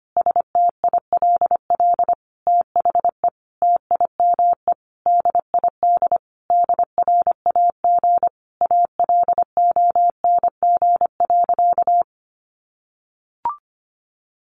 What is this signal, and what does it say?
Morse code, heard as a keyed tone, about 25 wpm, 700 Hz. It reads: STILL THE TIME DID DRAG ALONG.
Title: Sentences from Top 1500 Words 25wpm